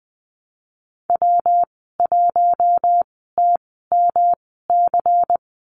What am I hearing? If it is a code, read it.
Morse code, heard as a keyed tone, about 19 wpm, spaced as for 14 wpm, 700 Hz. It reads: W1TMC
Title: 167 Lesson 27g - Number 7 - Call Signs 20wpm